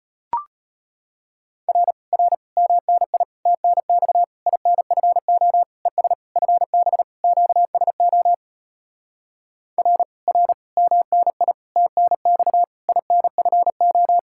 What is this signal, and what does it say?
Morse code, heard as a keyed tone, about 38 wpm, 700 Hz. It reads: R R MNI TNX INFO ES FB QSO R R MNI TNX INFO